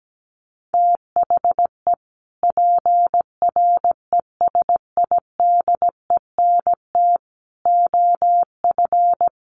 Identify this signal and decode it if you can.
Morse code, heard as a keyed tone, about 17 wpm, 700 Hz. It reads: THE PRESIDENT OF
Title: Sets of 4 Words 17wpm